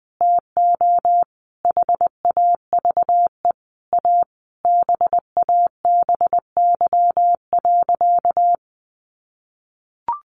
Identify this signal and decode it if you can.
Morse code, heard as a keyed tone, about 20 wpm, 700 Hz. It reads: TO HAVE A BABY.
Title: Sentences from Top 1300 Words 20wpm